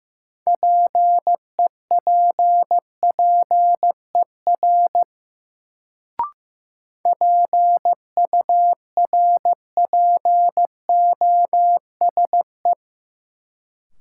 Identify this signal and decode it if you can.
Morse code, heard as a keyed tone, about 15 wpm, 700 Hz. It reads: PEPPER PURPOSE
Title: Sets of 5 Words - New Word Drill - 15wpm